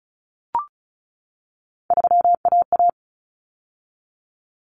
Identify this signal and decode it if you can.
Morse code, heard as a keyed tone, about 34 wpm, 700 Hz. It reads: 3AA
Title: Common Cut Numbers - 3-Digit 35wpm